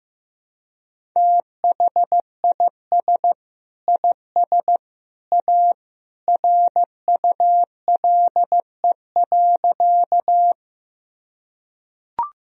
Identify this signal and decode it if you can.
Morse code, heard as a keyed tone, about 15 wpm, 700 Hz. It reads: THIS IS A RULE.
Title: Sentences from Top 1000 Words - Review Part 2 - 15wpm